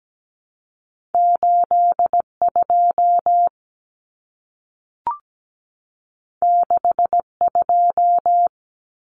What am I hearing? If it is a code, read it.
Morse code, heard as a keyed tone, about 17 wpm, 700 Hz. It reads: 82 62
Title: Common Cut Numbers - 2-Digit 17wpm